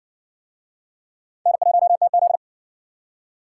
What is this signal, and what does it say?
Morse code, heard as a keyed tone, about 60 wpm, 700 Hz. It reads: N1TZ